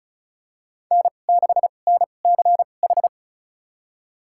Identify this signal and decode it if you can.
Morse code, heard as a keyed tone, about 36 wpm, spaced as for 25 wpm, 700 Hz. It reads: N6NCH
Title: USA Call Signs - 5 Characters - 2x Character Spacing - 35wpm